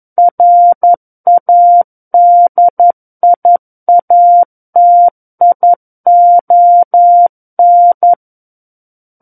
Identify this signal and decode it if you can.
Morse code, heard as a keyed tone, about 11 wpm, 700 Hz. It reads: RADIATION
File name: 11 WPM